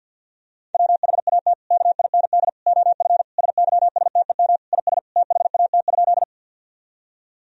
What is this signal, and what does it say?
Morse code, heard as a keyed tone, about 52 wpm, 700 Hz. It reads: WHAT KIND OF SYSTEM IS THAT?
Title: Sentences from Top 1000 Words - Review Part 3 - 50wpm